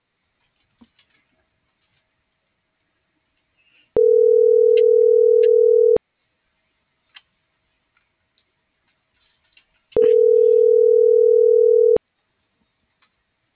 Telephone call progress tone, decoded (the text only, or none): ringback tone